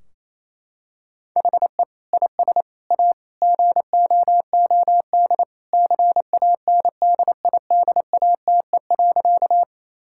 Morse code, heard as a keyed tone, about 28 wpm, 700 Hz. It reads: HE IS A GOOD CANDIDATE.
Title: Sentences from Top 1600 Words 28wpm